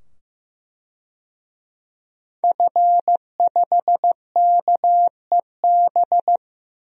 Morse code, heard as a keyed tone, about 15 wpm, 700 Hz. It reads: F5KEB
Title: DXCC-100 Call Signs - 5 Characters - 15wpm